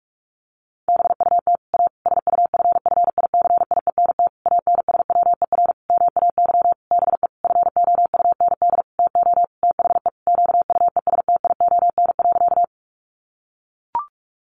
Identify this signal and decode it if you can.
Morse code, heard as a keyed tone, about 44 wpm, 700 Hz. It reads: BUT A SUFFICIENT ANSWER MAY BE FOUND TO THE QUESTION.
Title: Sentences from Top 1400 Words 45wpm